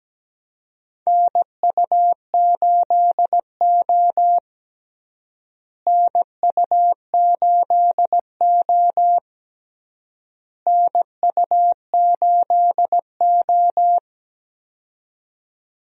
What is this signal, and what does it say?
Morse code, heard as a keyed tone, about 17 wpm, 700 Hz. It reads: NU8O NU8O NU8O
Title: USA Call Signs - 4 Characters - Repeated 3x - 17wpm